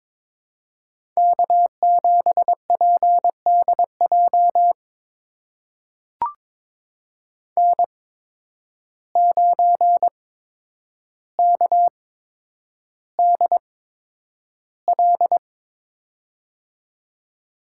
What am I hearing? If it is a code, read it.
Morse code, heard as a keyed tone, about 22 wpm, spaced as for 9 wpm, 700 Hz. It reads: K7PDJ N9KDL
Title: USA Call Signs - 5 Characters - 8x Character Spacing - 22wpm